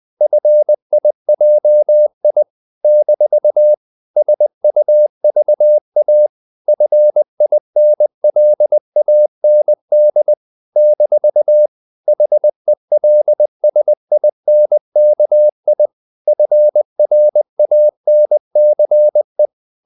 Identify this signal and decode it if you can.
Morse code, heard as a keyed tone, about 20 wpm, 600 Hz. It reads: FIJI - SUVA FINLAND - HELSINKI FRANCE
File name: CW / Practice / NationalCapitals20